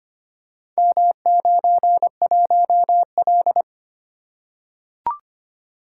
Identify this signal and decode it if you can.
Morse code, heard as a keyed tone, about 25 wpm, 690 Hz. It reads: M91L
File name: practice / 25wpm / Four Letter-Number Combinations - 25wpm